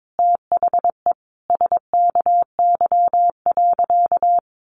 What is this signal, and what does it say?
Morse code, heard as a keyed tone, about 22 wpm, 700 Hz. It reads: THE SKY.
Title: Sentences from Top 1200 Words 22wpm